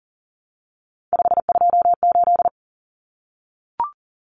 Morse code, heard as a keyed tone, about 40 wpm, 700 Hz. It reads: 528
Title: Common Cut Numbers - 3-Digit 40wpm